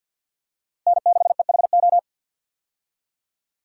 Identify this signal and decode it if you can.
Morse code, heard as a keyed tone, about 48 wpm, 700 Hz. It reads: N6EHO
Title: USA Call Signs - 5 Characters - Repeated 2x - 50wpm